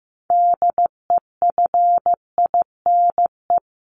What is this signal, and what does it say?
Morse code, heard as a keyed tone, about 15 wpm, 700 Hz. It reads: DEFINE
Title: ED Suffix - Top 100 Words with Base Words 15wpm